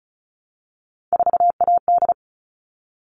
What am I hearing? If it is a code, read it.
Morse code, heard as a keyed tone, about 34 wpm, 700 Hz. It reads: 4AD
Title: All Cut Numbers - 3-Digit 35wpm